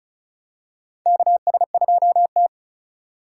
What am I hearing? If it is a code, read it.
Morse code, heard as a keyed tone, about 36 wpm, 700 Hz. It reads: KS2T
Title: USA Call Signs - 4 Characters - 3x Character Spacing - 35wpm